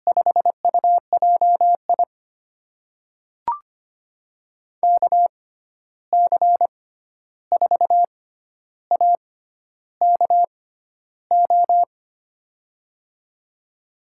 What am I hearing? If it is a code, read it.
Morse code, heard as a keyed tone, about 25 wpm, spaced as for 11 wpm, 700 Hz. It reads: 5UJI KC4AKO
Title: USA Call Signs - 6x Character Spacing - 25wpm